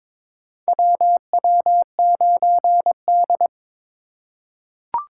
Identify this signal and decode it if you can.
Morse code, heard as a keyed tone, about 22 wpm, 700 Hz. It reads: WW9D
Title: USA Call Signs - 4x Character Spacing - 22wpm